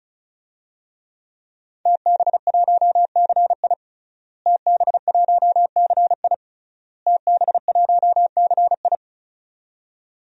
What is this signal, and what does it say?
Morse code, heard as a keyed tone, about 34 wpm, 690 Hz. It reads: TB1CI TB1CI TB1CI
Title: DXCC-100 Call Signs - 5 Characters - Repeated 3x - 35wpm